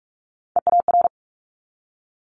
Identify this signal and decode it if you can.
Morse code, heard as a keyed tone, about 46 wpm, 700 Hz. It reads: EAR
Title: Multi-letter Phonemes 45wpm